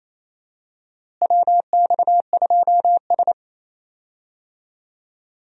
Morse code, heard as a keyed tone, about 28 wpm, 700 Hz. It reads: WX2S
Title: USA Call Signs - 4 Characters - Repeated 3x - 28wpm